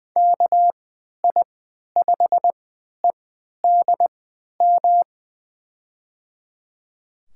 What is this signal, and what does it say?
Morse code, heard as a keyed tone, about 20 wpm, spaced as for 11 wpm, 700 Hz. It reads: KI5EDM
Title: USA Call Signs - 6 Characters - 3x Character Spacing - 20wpm